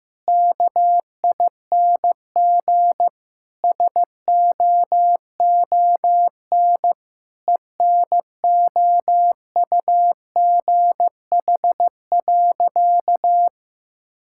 Morse code, heard as a keyed tone, about 16 wpm, 700 Hz. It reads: KING SOON ENOUGH.
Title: Sentences from Top 1000 Words - Review Part 6 - 15wpm